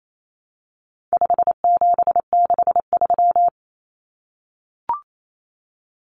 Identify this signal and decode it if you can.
Morse code, heard as a keyed tone, about 28 wpm, 700 Hz. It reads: 5763
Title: Numbers - 4-Digit 28wpm